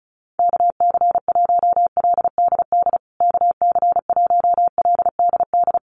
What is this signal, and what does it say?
Morse code, heard as a keyed tone, about 34 wpm, 700 Hz. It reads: KC1LDD KC1LDD
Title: USA Call Signs Repeated 2x 35wpm